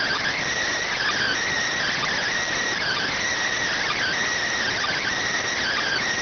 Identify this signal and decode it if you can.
Emergency Alert System (SAME) tones, heard as harsh buzzing bursts. It is heard from 0.9 to 5.1 s.